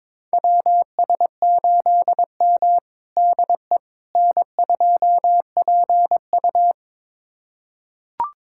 Morse code, heard as a keyed tone, about 22 wpm, 700 Hz. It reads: WS8M DE N2PU
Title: USA Callsign DE Callsign - 4 Characters - 22wpm